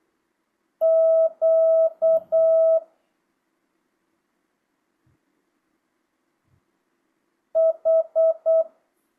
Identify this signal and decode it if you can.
Morse code, heard as a keyed tone, about 8 wpm, 650 Hz. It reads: Q H
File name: jh8cbh / 2025cw